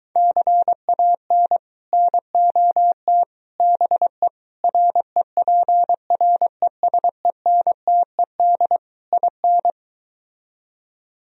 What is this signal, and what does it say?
Morse code, heard as a keyed tone, about 23 wpm, 700 Hz. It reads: CAN NOT BE REPRESENTED IN